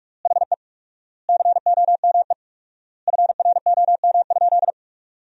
Morse code, heard as a keyed tone, about 44 wpm, 700 Hz. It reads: HE COME FROM?
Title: 4x Word-Spacing - Sentences from Top 300 Words - 45wpm